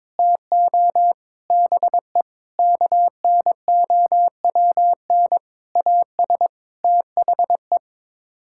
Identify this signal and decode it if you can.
Morse code, heard as a keyed tone, about 22 wpm, 700 Hz. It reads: TO BE KNOWN AS THE
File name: practice / 22wpm / Sets of 8 Words - 22wpm